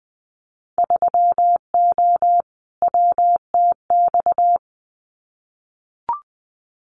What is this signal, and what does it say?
Morse code, heard as a keyed tone, about 21 wpm, 700 Hz. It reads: 3O WTX
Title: Winter Field Day Exchanges Encore 20wpm